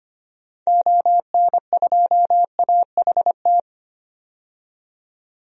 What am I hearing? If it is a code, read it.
Morse code, heard as a keyed tone, about 25 wpm, 700 Hz. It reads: ON2AHT
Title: DXCC-100 Call Signs - 6 Characters - 25wpm